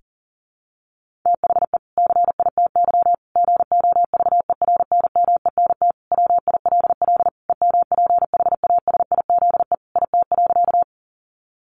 Morse code, heard as a keyed tone, about 40 wpm, 700 Hz. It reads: THE CITY GOVERNMENT WILL EMPHASIZE IT.
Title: Sentences from Top 1500 Words 40wpm